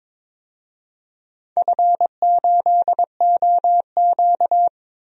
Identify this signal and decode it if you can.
Morse code, heard as a keyed tone, about 22 wpm, 700 Hz. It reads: F8OQ
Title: DXCC-100 Call Signs - 4 Characters - 22wpm